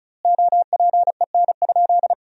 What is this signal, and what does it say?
Morse code, heard as a keyed tone, about 34 wpm, 700 Hz. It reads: OPEN?
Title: 4x Word-Spacing - Sentences from Top 300 Words - 35wpm